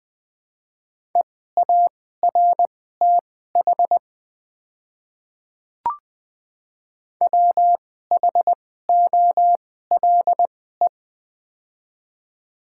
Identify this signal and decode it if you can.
Morse code, heard as a keyed tone, about 20 wpm, spaced as for 15 wpm, 700 Hz. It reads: EARTH WHOLE
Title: 2x Character Spacing - Top 500 Words - 20wpm